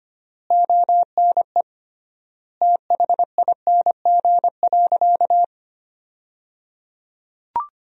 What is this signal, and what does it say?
Morse code, heard as a keyed tone, about 25 wpm, 700 Hz. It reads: ONE THING.
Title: 3x Word-Spacing - Sentences from Top 300 Words - 25wpm